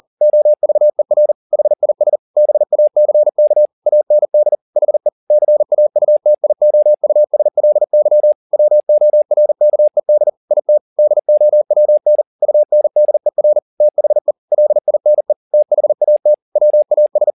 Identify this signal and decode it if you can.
Morse code, heard as a keyed tone, about 40 wpm, 600 Hz. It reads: OVER HIS BACK AND HE CAUTIOUSLY WORKED IT DOWN UNDER THE LINE THAT WAS